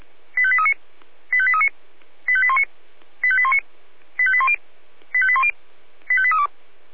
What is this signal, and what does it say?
Five-tone selective call (ZVEI), heard as quick runs of five tones, tones of about 70 ms. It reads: 85839 85839 85819 85819 85810 85810 85832
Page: /496/SDR/ZVEI/85882/2023.10.06/